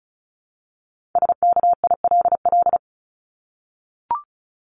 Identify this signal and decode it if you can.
Morse code, heard as a keyed tone, about 34 wpm, 700 Hz. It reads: SKILL
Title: Top 1001-1250 Words 35wpm